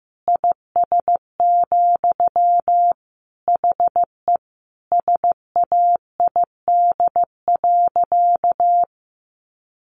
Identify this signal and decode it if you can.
Morse code, heard as a keyed tone, about 15 wpm, 700 Hz. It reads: IS, HE SAID.
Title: Sentences from Top 200 Words 15wpm